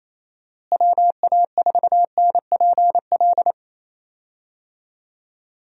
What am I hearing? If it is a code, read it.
Morse code, heard as a keyed tone, about 28 wpm, 700 Hz. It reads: WA4NPL
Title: USA Call Signs - 6 Characters - Repeated 3x - 28wpm